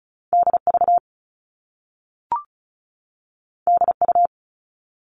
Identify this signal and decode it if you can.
Morse code, heard as a keyed tone, about 36 wpm, 690 Hz. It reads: DV DU
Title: All Cut Numbers - 2-Digit 35wpm